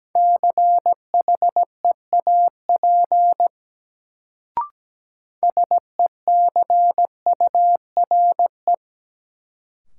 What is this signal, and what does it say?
Morse code, heard as a keyed tone, about 17 wpm, 700 Hz. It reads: CHEAP SECURE
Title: Sentences from Top 1200 Words - New Word Drill Alt - 17wpm